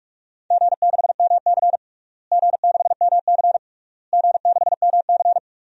Morse code, heard as a keyed tone, about 41 wpm, 700 Hz. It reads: G6MC G6MC G6MC
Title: DXCC-100 Call Signs - 4 Characters - Repeated 3x - 45wpm